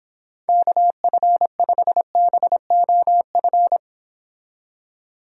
Morse code, heard as a keyed tone, about 29 wpm, 700 Hz. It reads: KF5BOF